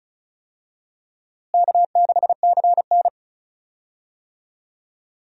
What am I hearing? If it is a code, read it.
Morse code, heard as a keyed tone, about 36 wpm, 700 Hz. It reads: K6CN
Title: USA Call Signs - 4 Characters - Repeated 3x - 35wpm